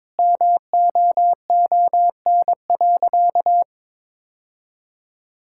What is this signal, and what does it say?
Morse code, heard as a keyed tone, about 22 wpm, 690 Hz. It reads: MOON.
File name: practice / 22wpm / Easy Sentences with Hints - 22wpm